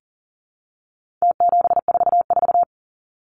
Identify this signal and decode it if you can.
Morse code, heard as a keyed tone, about 40 wpm, 700 Hz. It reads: T744
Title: Numbers - Contest Style - 4-Digit 40wpm